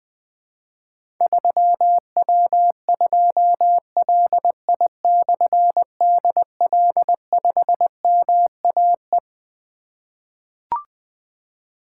Morse code, heard as a keyed tone, about 20 wpm, 700 Hz. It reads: 3W2LI/DL5MAE
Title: Difficult Call Signs 20wpm